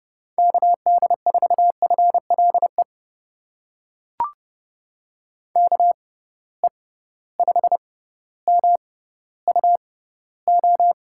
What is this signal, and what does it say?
Morse code, heard as a keyed tone, about 30 wpm, spaced as for 15 wpm, 700 Hz. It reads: KD4FLE KE5MUO
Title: USA Call Signs - 6 Characters - 6x Character Spacing - 30wpm